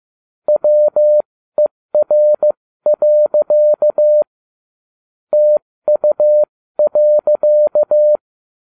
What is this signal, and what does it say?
Morse code, heard as a keyed tone, about 15 wpm, spaced as for 13 wpm, 600 Hz. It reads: WER. TU.